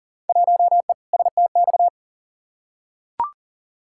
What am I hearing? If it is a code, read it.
Morse code, heard as a keyed tone, about 39 wpm, 700 Hz. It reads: 1E STX
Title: Field Day Exchanges - 40wpm